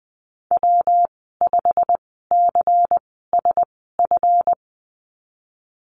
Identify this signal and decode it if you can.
Morse code, heard as a keyed tone, about 19 wpm, spaced as for 14 wpm, 700 Hz. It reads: W5CSF
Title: 167 Lesson 27g - Number 7 - Call Signs 20wpm